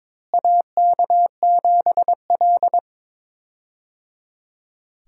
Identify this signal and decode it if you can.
Morse code, heard as a keyed tone, about 22 wpm, 700 Hz. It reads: AK7L
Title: USA Call Signs - 4 Characters - Repeated 3x - 22wpm